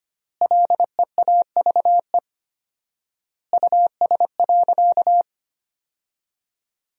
Morse code, heard as a keyed tone, about 25 wpm, 700 Hz. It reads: LEAVE US.
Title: 4x Word-Spacing - Sentences from Top 400 Words - 25wpm